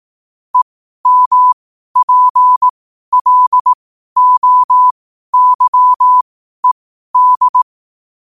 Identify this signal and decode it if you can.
Morse code, heard as a keyed tone, about 18 wpm, spaced as for 12 wpm, 1000 Hz. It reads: EMPLOYED
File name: qotdfiles / 12wpm